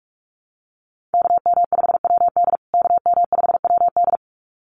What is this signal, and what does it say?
Morse code, heard as a keyed tone, about 44 wpm, 700 Hz. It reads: KM5WD KM5WD
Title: USA Call Signs Repeated 2x 45wpm